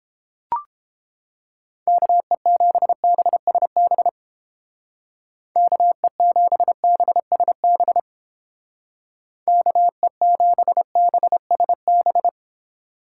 Morse code, heard as a keyed tone, about 31 wpm, 700 Hz. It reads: KE7BSB KE7BSB KE7BSB